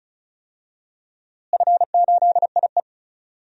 Morse code, heard as a keyed tone, about 36 wpm, 700 Hz. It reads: F8IE
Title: DXCC-100 Call Signs - 4 Characters - Repeated 2x - 35wpm